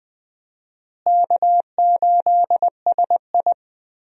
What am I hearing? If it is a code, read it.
Morse code, heard as a keyed tone, about 21 wpm, 700 Hz. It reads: K8SI
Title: USA Call Signs - 4 Characters - Repeated 3x - 20wpm